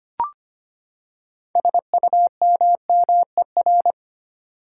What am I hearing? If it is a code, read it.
Morse code, heard as a keyed tone, about 25 wpm, 700 Hz. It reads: SUMMER
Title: Six Letter Words - 25wpm